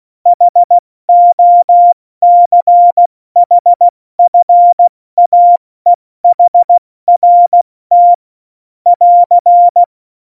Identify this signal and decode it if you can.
Morse code, heard as a keyed tone, about 16 wpm, 700 Hz. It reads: HOCHFAEHRT <AR>
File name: Morsekurs / Witze / Witz 15. Januar 2019, 70 F80